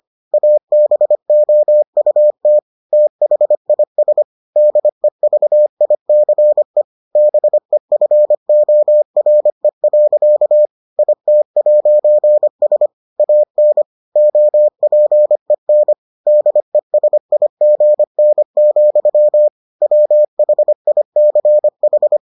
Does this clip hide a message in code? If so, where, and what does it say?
Yes: Morse code from 0.3 to 22.2 s, heard as a keyed tone, about 25 wpm, 600 Hz. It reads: ABOUT THIS DEVICE BEFORE. IT'S AN OPEN DESIGN, WHICH